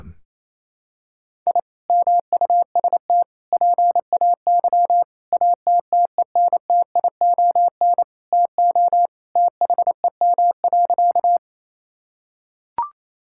Morse code, heard as a keyed tone, about 28 wpm, 700 Hz. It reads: I MUST PAY ATTENTION TO THEM.
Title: Sentences from Top 1000 Words - Review Part 6 - 28wpm